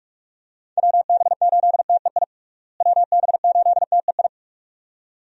Morse code, heard as a keyed tone, about 46 wpm, 700 Hz. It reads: WB8TEI WB8TEI
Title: USA Call Signs - 6 Characters - Repeated 2x - 45wpm